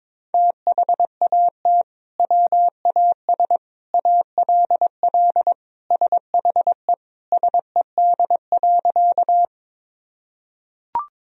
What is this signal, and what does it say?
Morse code, heard as a keyed tone, about 21 wpm, 700 Hz. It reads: THAT WAS ALL SHE SED.
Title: Sentences from Top 100 Words and Abbreviations 22wpm